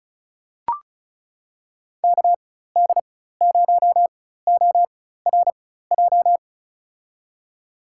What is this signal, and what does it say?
Morse code, heard as a keyed tone, about 36 wpm, spaced as for 16 wpm, 700 Hz. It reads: KD0ORJ
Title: USA Call Signs - 6 Characters - 4x Character Spacing - 35wpm